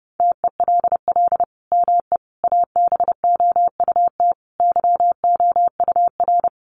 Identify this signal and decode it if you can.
Morse code, heard as a keyed tone, about 30 wpm, 700 Hz. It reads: TELL ME ABOUT YOUR
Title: Sentences from Top 1100 Words 30wpm